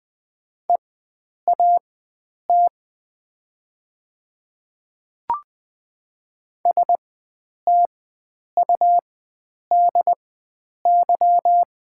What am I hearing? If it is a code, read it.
Morse code, heard as a keyed tone, about 20 wpm, spaced as for 9 wpm, 690 Hz. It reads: EAT STUDY
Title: 4x Character Spacing - Top 300 Words - 20wpm